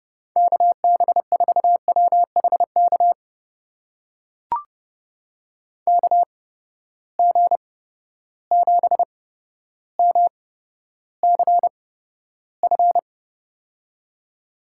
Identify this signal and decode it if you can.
Morse code, heard as a keyed tone, about 30 wpm, spaced as for 13 wpm, 700 Hz. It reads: KB4WHK KG7MCF
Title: USA Call Signs - 6 Characters - 8x Character Spacing - 30wpm